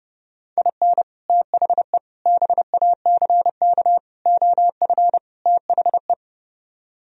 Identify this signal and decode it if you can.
Morse code, heard as a keyed tone, about 36 wpm, 700 Hz. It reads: IN THE BACK OF THE
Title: Speed-Racing - Sets of 5 Words 30wpm